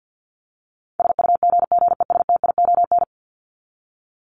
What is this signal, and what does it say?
Morse code, heard as a keyed tone, about 50 wpm, 700 Hz. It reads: SUGGESTION
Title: Top 1501-1750 Words 50wpm